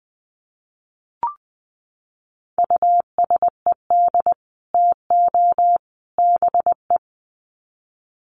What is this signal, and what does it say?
Morse code, heard as a keyed tone, about 20 wpm, 700 Hz. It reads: USED TO BE
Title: Sets of 3 Words 20wpm